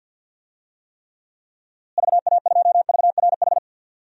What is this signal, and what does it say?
Morse code, heard as a keyed tone, about 51 wpm, 700 Hz. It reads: VA2VRH
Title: DXCC-100 Call Signs - 6 Characters - Repeated 3x - 50wpm